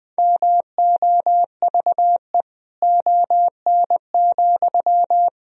Morse code, heard as a keyed tone, about 20 wpm, 700 Hz. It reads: MOVE ON,
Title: Sentences from Top 500 Words and Abbreviations 20wpm